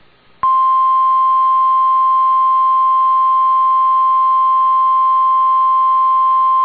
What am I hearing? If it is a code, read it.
Emergency Alert System (SAME) tones (the 1050 Hz attention tone). It is heard from 1.1 to 6.0 s.